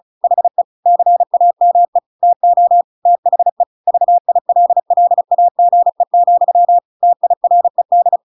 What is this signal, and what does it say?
Morse code, heard as a keyed tone, about 36 wpm, 700 Hz. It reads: HE CAME TO THE VILLAGE, TIRED
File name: Pinocchio / Pinocchio ch6 35 35 ik1hge